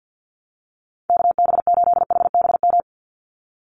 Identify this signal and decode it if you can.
Morse code, heard as a keyed tone, about 50 wpm, 700 Hz. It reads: KB8HBM
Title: USA Call Signs 50wpm